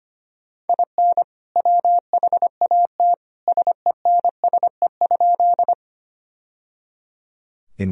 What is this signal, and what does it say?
Morse code, heard as a keyed tone, about 25 wpm, 700 Hz. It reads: IN WHAT SENSE?
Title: Sentences from Top 1000 Words - Review Part 4 - 25wpm